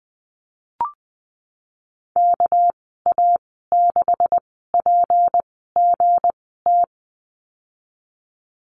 Morse code, heard as a keyed tone, about 19 wpm, spaced as for 14 wpm, 700 Hz. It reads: KA6PGT
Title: 207 Lesson 33g - Letter j - Call Signs 20wpm